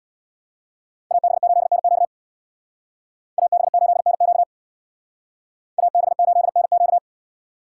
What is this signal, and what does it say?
Morse code, heard as a keyed tone, about 64 wpm, 700 Hz. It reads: AB8NQ AB8NQ AB8NQ